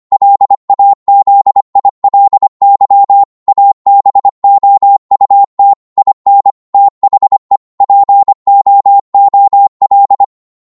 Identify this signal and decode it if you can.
Morse code, heard as a keyed tone, about 25 wpm, 800 Hz. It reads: LAZILY ABOUT IN THE POOL